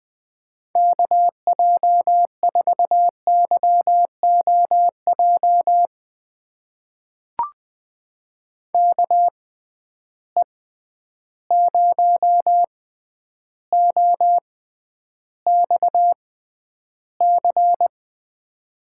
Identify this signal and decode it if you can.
Morse code, heard as a keyed tone, about 20 wpm, spaced as for 10 wpm, 700 Hz. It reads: KJ4YOJ KE0OXC
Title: USA Call Signs - 6 Characters - 6x Character Spacing - 20wpm